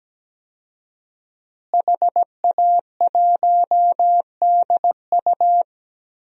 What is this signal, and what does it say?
Morse code, heard as a keyed tone, about 17 wpm, 700 Hz. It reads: HA1DU
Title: DXCC-100 Call Signs - 5 Characters - Repeated 2x - 17wpm